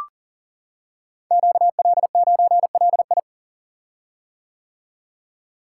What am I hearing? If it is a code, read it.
Morse code, heard as a keyed tone, about 38 wpm, 700 Hz. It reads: QL9LI
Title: Code Groups - 5 Letter:Numbers - 40wpm